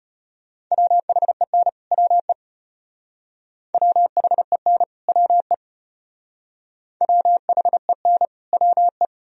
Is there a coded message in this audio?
Yes: Morse code from 0.7 to 9.1 s, heard as a keyed tone, about 32 wpm, 700 Hz. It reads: WHEN WE WHEN WE WHEN WE